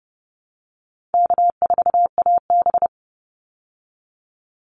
Morse code, heard as a keyed tone, about 30 wpm, 700 Hz. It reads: K4AB
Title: Call Signs Repeated 2x - 30wpm